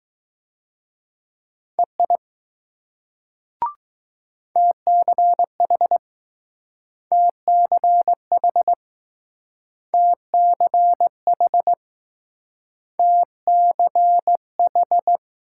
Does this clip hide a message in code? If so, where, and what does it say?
Yes: Morse code from 1.8 to 15.2 s, heard as a keyed tone, about 20 wpm, 700 Hz. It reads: EI TCH TCH TCH TCH